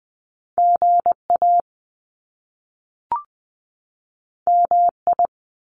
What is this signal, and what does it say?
Morse code, heard as a keyed tone, about 20 wpm, 700 Hz. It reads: GA MI
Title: US States 20wpm